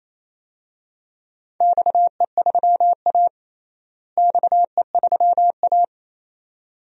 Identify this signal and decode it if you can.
Morse code, heard as a keyed tone, about 28 wpm, 700 Hz. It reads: XE3A XE3A
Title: DXCC-100 Call Signs - 4 Characters - Repeated 2x - 28wpm